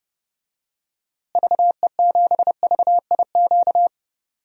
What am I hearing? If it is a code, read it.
Morse code, heard as a keyed tone, about 32 wpm, 700 Hz. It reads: VE7VIQ
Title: DXCC-100 Call Signs - 6 Characters - Repeated 2x - 30wpm